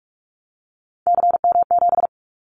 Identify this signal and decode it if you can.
Morse code, heard as a keyed tone, about 46 wpm, 700 Hz. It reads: CM7
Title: Callsign Prefixes for Top 100 DXCC - 45wpm